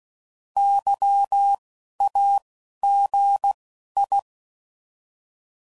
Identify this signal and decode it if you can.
Morse code, heard as a keyed tone, about 16 wpm, spaced as for 12 wpm, 800 Hz. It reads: YAGI
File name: CW Code Groups @10WPM / fromSM's / Antennas